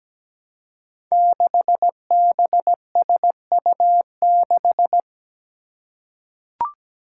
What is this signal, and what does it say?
Morse code, heard as a keyed tone, about 17 wpm, 700 Hz. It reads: 6BSU6
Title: Five Challenging Letter-Number Combinations 17wpm